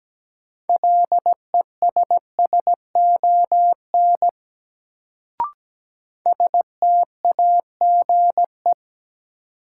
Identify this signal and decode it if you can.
Morse code, heard as a keyed tone, about 17 wpm, 700 Hz. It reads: LESSON STAGE
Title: Sentences from Top 1200 Words - New Word Drill - 17wpm